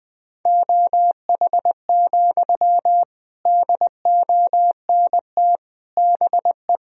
Morse code, heard as a keyed tone, about 20 wpm, 700 Hz. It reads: OH, DONT BE
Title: Sentences from Top 1000 Words - Review Part 1 - 20wpm